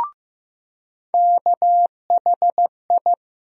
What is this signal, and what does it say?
Morse code, heard as a keyed tone, about 15 wpm, 700 Hz. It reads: KHI
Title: USA Call Sign Postfixes - 3 Letters 15wpm